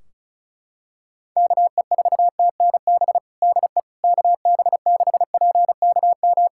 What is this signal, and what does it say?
Morse code, heard as a keyed tone, about 36 wpm, 700 Hz. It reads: KE4TNB DE KB6PKM
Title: Practice QSO - Exchange 4 - Ending - 35wpm